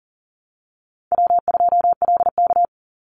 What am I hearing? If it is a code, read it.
Morse code, heard as a keyed tone, about 40 wpm, 700 Hz. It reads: W2LK
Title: Call Signs Repeated 2x - 40wpm